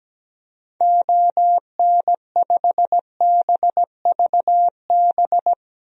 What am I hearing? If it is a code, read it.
Morse code, heard as a keyed tone, about 17 wpm, 700 Hz. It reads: ON5BVB
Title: DXCC-100 Call Signs - 6 Characters - Repeated 3x - 17wpm